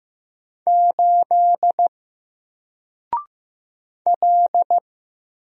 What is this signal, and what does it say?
Morse code, heard as a keyed tone, about 15 wpm, 700 Hz. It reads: 8 L
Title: Single Letter-Number 15wpm